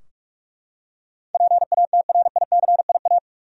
Morse code, heard as a keyed tone, about 45 wpm, 700 Hz. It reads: PATRICIA
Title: USA Common Names - 45wpm